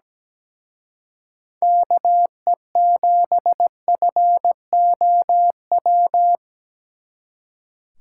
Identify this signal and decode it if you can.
Morse code, heard as a keyed tone, about 20 wpm, 700 Hz. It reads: KE7FOW
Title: Speed Racing - USA 2x3 Call Signs 17wpm